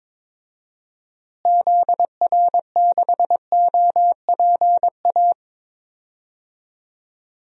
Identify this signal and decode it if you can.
Morse code, heard as a keyed tone, about 22 wpm, 700 Hz. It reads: ZR6OPA
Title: DXCC-100 Call Signs - 6 Characters - Repeated 2x - 22wpm